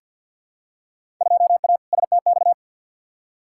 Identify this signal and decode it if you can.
Morse code, heard as a keyed tone, about 46 wpm, 700 Hz. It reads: 2A STX
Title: Field Day Exchanges - 50wpm